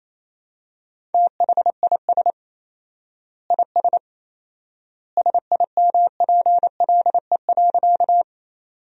Morse code, heard as a keyed tone, about 28 wpm, 700 Hz. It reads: THIS IS SIMPLE.
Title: 4x Word-Spacing - Sentences from Top 400 Words - 28wpm